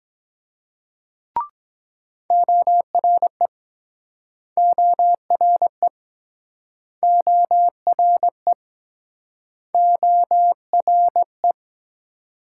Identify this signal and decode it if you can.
Morse code, heard as a keyed tone, about 24 wpm, 700 Hz. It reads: ORE ORE ORE ORE